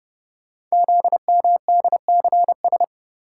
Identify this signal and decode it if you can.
Morse code, heard as a keyed tone, about 30 wpm, 700 Hz. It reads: ZMDCS
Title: Code Groups - 5 Letters - 30wpm